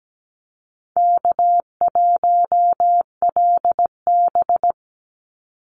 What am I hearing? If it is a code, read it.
Morse code, heard as a keyed tone, about 17 wpm, 700 Hz. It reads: K1LB
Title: NA SOTA Activator Callsigns 17wpm